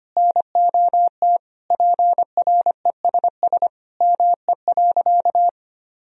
Morse code, heard as a keyed tone, about 25 wpm, 700 Hz. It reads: NOT PRESS ME.
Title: Sentences from Top 1000 Words - Review Part 1 - 25wpm